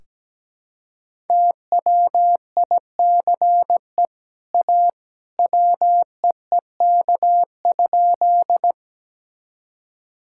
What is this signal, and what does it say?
Morse code, heard as a keyed tone, about 17 wpm, 700 Hz. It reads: TWICE A WEEK?
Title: Sentences from Top 1600 Words 17wpm